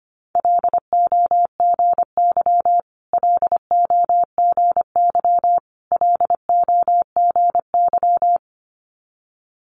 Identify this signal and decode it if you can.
Morse code, heard as a keyed tone, about 25 wpm, 700 Hz. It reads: LOGY LOGY LOGY
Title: Multi-letter Phonemes Repeated 3x 25wpm